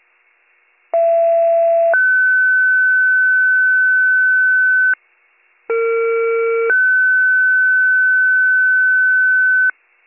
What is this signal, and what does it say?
Two-tone page, heard as a one-second tone, then a three-second tone, about 670 then 1500 Hz, about 460 then 1500 Hz. It is heard from 0.9 to 9.7 s.